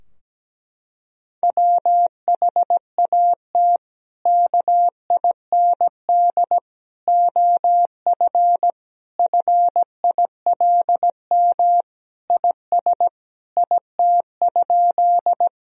Morse code, heard as a keyed tone, about 17 wpm, 700 Hz. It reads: WHAT KIND OF FILM IS IT?
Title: Sentences from Top 1000 Words - Review Part 3 - 17wpm